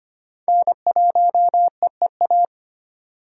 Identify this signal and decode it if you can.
Morse code, heard as a keyed tone, about 33 wpm, 700 Hz. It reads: N1EEA